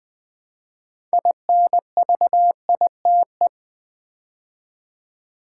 Morse code, heard as a keyed tone, about 20 wpm, 700 Hz. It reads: INVITE
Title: Build A Word - 2 Letters At A Time - 6-Letter Words - 20wpm